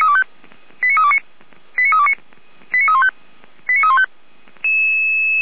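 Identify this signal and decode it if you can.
Five-tone selective call (ZVEI), heard as quick runs of five tones, tones of about 70 ms. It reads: 89329 89329 89315 89315
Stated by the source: Five-tone ZVEI selective call received on SDR.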